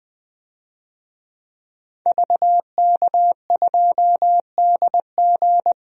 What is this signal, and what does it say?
Morse code, heard as a keyed tone, about 20 wpm, 700 Hz. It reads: VK2DG
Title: DXCC-100 Call Signs - 5 Characters - 20wpm